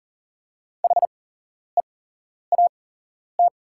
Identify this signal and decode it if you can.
Morse code, heard as a keyed tone, about 39 wpm, spaced as for 11 wpm, 690 Hz. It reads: HEAT
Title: 8x Character Spacing - Top 500 Words - 40wpm